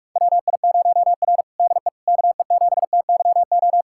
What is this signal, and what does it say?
Morse code, heard as a keyed tone, about 46 wpm, 700 Hz. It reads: WI0R DE KE7TYO
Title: USA Callsign DE Callsign 45wpm